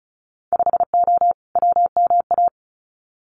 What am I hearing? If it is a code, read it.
Morse code, heard as a keyed tone, about 36 wpm, 700 Hz. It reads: 5O WMA
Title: Winter Field Day Exchanges Encore 35wpm